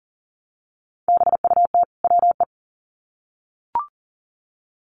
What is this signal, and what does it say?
Morse code, heard as a keyed tone, about 40 wpm, 700 Hz. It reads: BUT WE
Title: Sets of 2 Words 40wpm